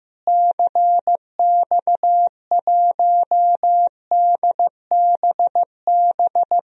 Morse code, heard as a keyed tone, about 15 wpm, 700 Hz. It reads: CX1DBB
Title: DXCC-100 Call Signs - 6 Characters - Repeated 2x - 15wpm